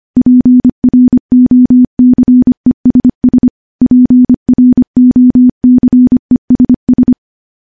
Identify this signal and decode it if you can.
Morse code, heard as a keyed tone, about 25 wpm, 250 Hz. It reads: PROCESS PROCESS
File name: morse / morse-training-25WPM